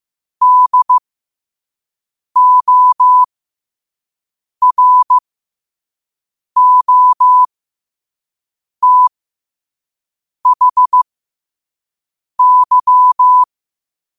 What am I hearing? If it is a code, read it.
Morse code, heard as a keyed tone, about 15 wpm, spaced as for 5 wpm, 1000 Hz. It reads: DOROTHY
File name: qotdfiles / 5wpm